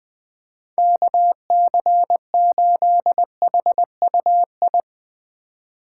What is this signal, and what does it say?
Morse code, heard as a keyed tone, about 21 wpm, 700 Hz. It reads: KC8HUI
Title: Speed Racing - USA 2x3 Call Signs 17wpm